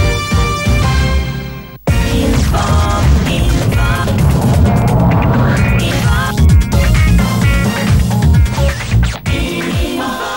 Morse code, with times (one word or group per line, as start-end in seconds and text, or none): none